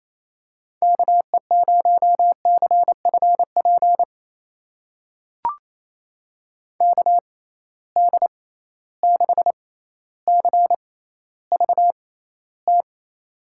Morse code, heard as a keyed tone, about 28 wpm, spaced as for 14 wpm, 700 Hz. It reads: KE0CFP KD6CVT
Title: USA Call Signs - 6 Characters - 6x Character Spacing - 28wpm